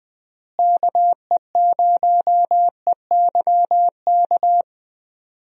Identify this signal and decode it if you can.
Morse code, heard as a keyed tone, about 20 wpm, 700 Hz. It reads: KE0EYK
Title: USA Call Signs - 6 Characters - Repeated 3x - 20wpm